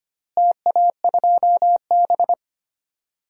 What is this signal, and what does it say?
Morse code, heard as a keyed tone, about 25 wpm, 700 Hz. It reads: TA2B
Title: DXCC-100 Call Signs - 4 Characters - Repeated 2x - 25wpm